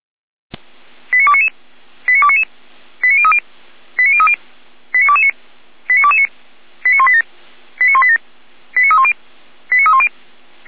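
Five-tone selective call (ZVEI), heard as quick runs of five tones, tones of about 70 ms. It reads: 89299 89299 89030 89030 89229 89229 89187 89187 89310 89310
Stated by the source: Five-tone ZVEI selective call received by SDR.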